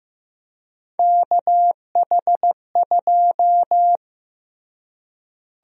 Morse code, heard as a keyed tone, about 15 wpm, 700 Hz. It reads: KH2
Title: USA Call Sign Prefixes - Repeated 2x - 15wpm